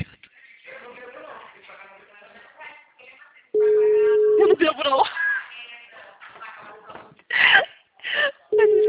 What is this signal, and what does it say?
Ringback tone (Europe, 425 Hz), with speech.